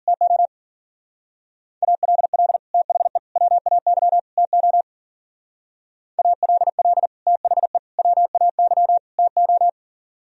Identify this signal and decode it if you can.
Morse code, heard as a keyed tone, about 49 wpm, 700 Hz. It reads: TO ALL THE WAY TO ALL THE WAY TO
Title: Speed-Racing - Sets of 4 Words 40wpm